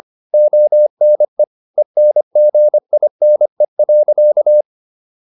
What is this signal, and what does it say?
Morse code, heard as a keyed tone, about 25 wpm, 600 Hz. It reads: ONE ENGINE.